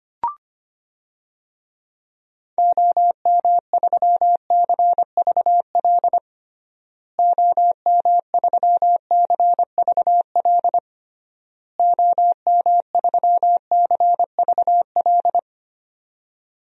Morse code, heard as a keyed tone, about 25 wpm, 700 Hz. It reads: OM3CVL OM3CVL OM3CVL
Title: DXCC-100 Call Signs - 6 Characters - Repeated 3x - 25wpm